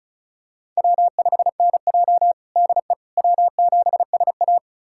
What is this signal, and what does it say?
Morse code, heard as a keyed tone, about 34 wpm, 700 Hz. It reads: W5NJ DE W7SA
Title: USA Callsign DE Callsign - 4 Characters - 35wpm